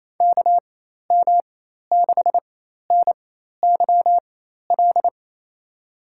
Morse code, heard as a keyed tone, about 28 wpm, spaced as for 13 wpm, 700 Hz. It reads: KM6NYL
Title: USA Call Signs - 6 Characters - 4x Character Spacing - 28wpm